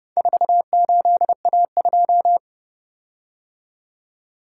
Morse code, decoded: 48A2

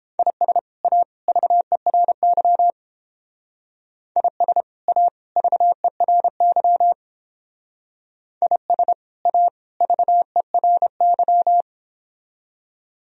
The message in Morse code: IS A VERY IS A VERY IS A VERY

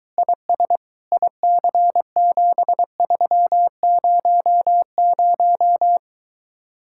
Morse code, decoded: IS IC7300

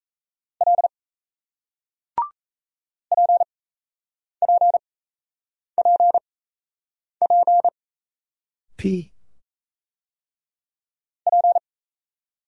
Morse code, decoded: L PPPP P

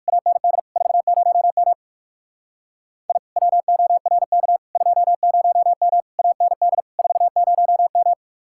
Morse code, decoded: AND 40M I WORK 20M AND 40M

